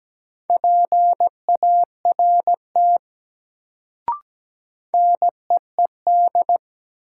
Morse code, heard as a keyed tone, about 17 wpm, 700 Hz. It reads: PART NEED
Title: Sentences from Top 200 Words - New Word Drill - 17wpm